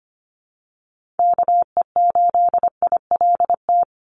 Morse code, heard as a keyed tone, about 25 wpm, 700 Hz. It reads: KE8ILT
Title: USA Call Signs 25wpm